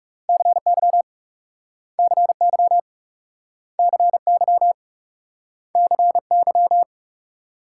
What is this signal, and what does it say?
Morse code, heard as a keyed tone, about 40 wpm, 700 Hz. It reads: CY CY CY CY